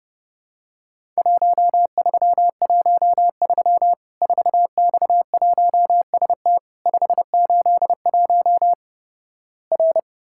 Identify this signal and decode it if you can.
Morse code, heard as a keyed tone, about 30 wpm, 700 Hz. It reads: 1313 4X1ST 581 R
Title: Worked All Europe DX Contest - QTC Exchange - 30wpm